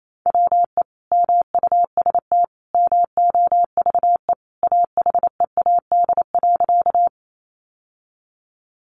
Morse code, heard as a keyed tone, about 28 wpm, 700 Hz. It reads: WE MUST MOVE AHEAD.
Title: Sentences from Top 1100 Words 28wpm